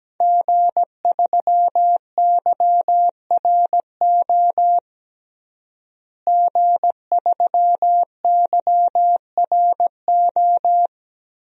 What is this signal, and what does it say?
Morse code, heard as a keyed tone, about 17 wpm, 700 Hz. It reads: G3YRO G3YRO
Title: DXCC-100 Call Signs - 5 Characters - Repeated 3x - 17wpm